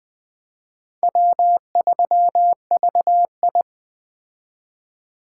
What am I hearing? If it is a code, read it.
Morse code, heard as a keyed tone, about 21 wpm, 700 Hz. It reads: W3VI
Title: Speed Racing - USA Call Signs 17wpm